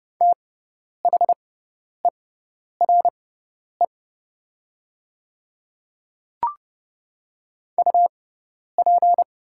Morse code, heard as a keyed tone, about 29 wpm, spaced as for 10 wpm, 700 Hz. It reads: THERE UP
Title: 6x Character Spacing - Top 100 Words - 30wpm